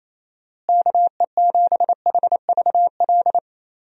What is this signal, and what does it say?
Morse code, heard as a keyed tone, about 28 wpm, 700 Hz. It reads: KE7HVL
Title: USA Call Signs - 6 Characters - Repeated 2x - 28wpm